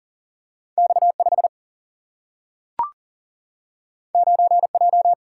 Morse code, heard as a keyed tone, about 40 wpm, 700 Hz. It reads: X5 9J
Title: Single-Letter Single-Number 40wpm